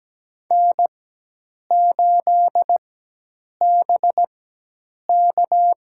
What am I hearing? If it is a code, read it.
Morse code, heard as a keyed tone, about 17 wpm, spaced as for 8 wpm, 700 Hz. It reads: N8BK
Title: USA Call Signs - 4 Characters - 4x Character Spacing - 17wpm